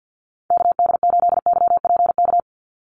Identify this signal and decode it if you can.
Morse code, heard as a keyed tone, about 50 wpm, 700 Hz. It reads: KD8YPK
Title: USA Call Signs 50wpm